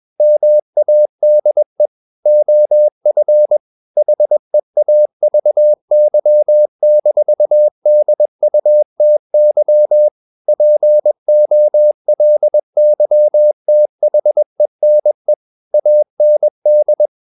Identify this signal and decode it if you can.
Morse code, heard as a keyed tone, about 21 wpm, 600 Hz. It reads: MADE OF HEAVY-DUTY POLYTHENE AND
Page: CW med Gnister